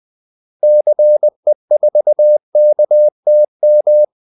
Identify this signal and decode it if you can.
Morse code, heard as a keyed tone, about 20 wpm, 600 Hz. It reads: CE4KTM
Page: Kaldesignaler 20 wpm